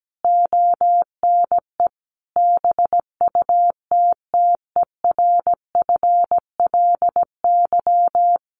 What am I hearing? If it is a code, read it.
Morse code, heard as a keyed tone, about 17 wpm, 700 Hz. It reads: ONE BUTTERFLY
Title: A Sound Of Thunder 17wpm